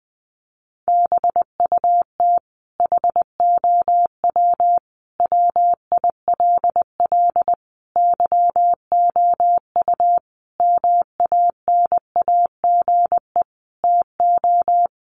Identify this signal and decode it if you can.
Morse code, heard as a keyed tone, about 20 wpm, 700 Hz. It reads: BUT HOW WILL YOU MANAGE TO
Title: Sentences from Top 1400 Words 20wpm